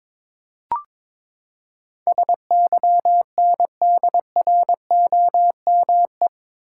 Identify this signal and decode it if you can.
Morse code, heard as a keyed tone, about 22 wpm, 700 Hz. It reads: SYNDROME
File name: practice / 22wpm / Eight Letter Words - 22wpm